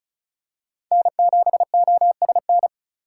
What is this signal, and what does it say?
Morse code, heard as a keyed tone, about 36 wpm, 690 Hz. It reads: N7OSN
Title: USA Call Signs - 4x Character Spacing - 35wpm